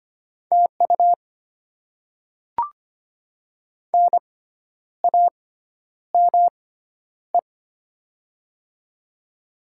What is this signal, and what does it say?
Morse code, heard as a keyed tone, about 25 wpm, spaced as for 10 wpm, 700 Hz. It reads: TU NAME
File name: Top 20 QSO Elements - 6x Character Spacing - 25wpm